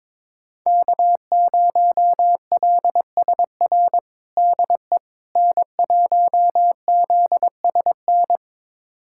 Morse code, heard as a keyed tone, about 22 wpm, 700 Hz. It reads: K0LSR DE N1ZSN
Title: USA Callsign DE Callsign - 5 Characters - 22wpm